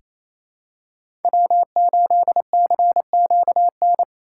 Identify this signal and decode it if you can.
Morse code, heard as a keyed tone, about 28 wpm, 700 Hz. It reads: W8CQN
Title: USA Call Signs - 5 Characters - 4x Character Spacing - 28wpm